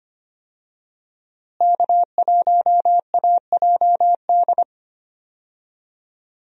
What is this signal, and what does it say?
Morse code, heard as a keyed tone, about 25 wpm, 700 Hz. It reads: K1AJD
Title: USA Call Signs - 5 Characters - Repeated 2x - 25wpm